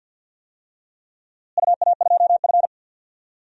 Morse code, heard as a keyed tone, about 49 wpm, 700 Hz. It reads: UA2F
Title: DXCC-100 Call Signs - 4 Characters - Repeated 3x - 50wpm